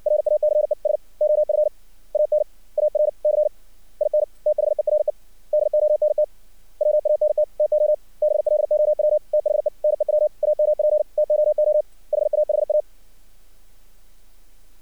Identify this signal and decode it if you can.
Morse code, heard as a keyed tone, about 55 wpm, 600 Hz. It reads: RACES OF AN ARC IN THERE DONT WANT TO BLOW THE NEW AMP TOO HAHA
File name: archived / CRSnet